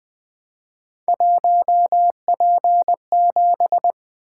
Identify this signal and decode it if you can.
Morse code, heard as a keyed tone, about 20 wpm, 700 Hz. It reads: 1P7
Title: Three Letter-Number Combinations - 20wpm